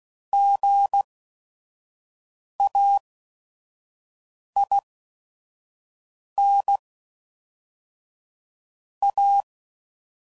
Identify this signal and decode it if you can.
Morse code, heard as a keyed tone, about 16 wpm, spaced as for 5 wpm, 800 Hz. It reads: GAIN A
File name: CW Code Groups @5WPM